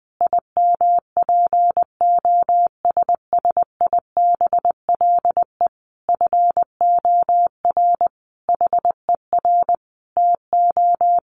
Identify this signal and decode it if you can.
Morse code, heard as a keyed tone, about 20 wpm, 700 Hz. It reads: IMPOSSIBLE FOR HER TO
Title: Sentences from Top 1100 Words 20wpm